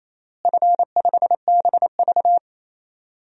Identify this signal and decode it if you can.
Morse code, heard as a keyed tone, about 28 wpm, 700 Hz. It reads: F5BV
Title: DXCC-100 Call Signs - 4 Characters - Repeated 3x - 28wpm